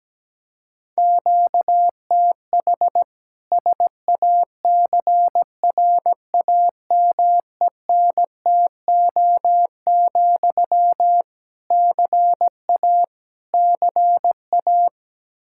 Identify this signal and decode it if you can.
Morse code, heard as a keyed tone, about 17 wpm, 700 Hz. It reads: QTH SACRAMENTO, CA CA
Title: K1USN SST QSO Practice 17wpm